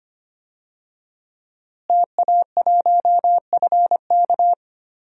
Morse code, heard as a keyed tone, about 25 wpm, 700 Hz. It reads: TA1FK
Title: DXCC-100 Call Signs - 5 Characters - Repeated 3x - 25wpm